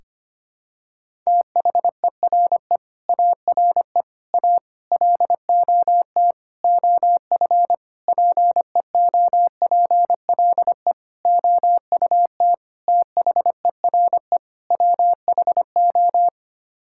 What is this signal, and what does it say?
Morse code, heard as a keyed tone, about 25 wpm, 700 Hz. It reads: THERE ARE A LOT OF PEOPLE OUT THERE WHO
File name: Sets of 9 Words - 25wpm